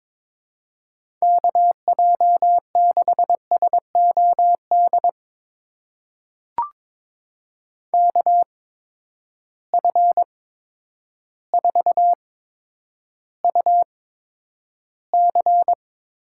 Morse code, heard as a keyed tone, about 22 wpm, spaced as for 10 wpm, 700 Hz. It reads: KJ6SOD KF4UC
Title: USA Call Signs - 6 Characters - 8x Character Spacing - 22wpm